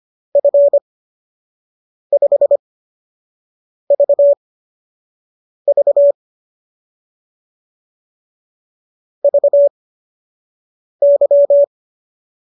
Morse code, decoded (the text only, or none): F5VV VY